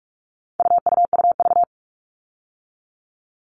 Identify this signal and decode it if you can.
Morse code, heard as a keyed tone, about 44 wpm, 700 Hz. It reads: UUUV
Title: All Cut Numbers - 4-Digit 45wpm